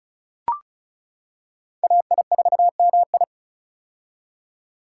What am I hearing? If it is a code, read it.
Morse code, heard as a keyed tone, about 36 wpm, 700 Hz. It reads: AI4MI
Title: K1USN SST QSO Practice - 35wpm